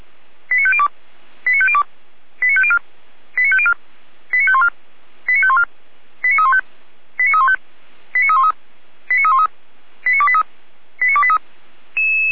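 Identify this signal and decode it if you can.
Five-tone selective call (ZVEI), heard as quick runs of five tones, tones of about 70 ms. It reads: 89502 89502 89504 89504 89514 89514 89315 89315 89313 89313 89283 89283
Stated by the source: ZVEI five-tone call 89201 received by SDR on the 455 channel.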